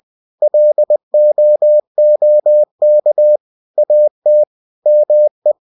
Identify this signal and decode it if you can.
Morse code, heard as a keyed tone, about 20 wpm, 600 Hz. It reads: LOOK AT ME
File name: winnie-20wpm